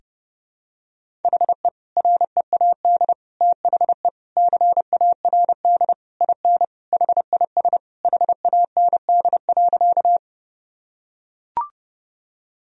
Morse code, heard as a keyed tone, about 32 wpm, 700 Hz. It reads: HE READ THE CARD IN HIS HAND.
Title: Sentences from Top 1000 Words - Review Part 4 - 30wpm